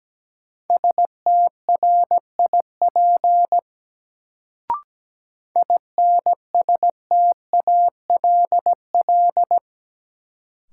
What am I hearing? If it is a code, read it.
Morse code, heard as a keyed tone, about 17 wpm, 700 Hz. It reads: STRIP INSTALL
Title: Sentences from Top 1600 Words - New Word Drill - 17wpm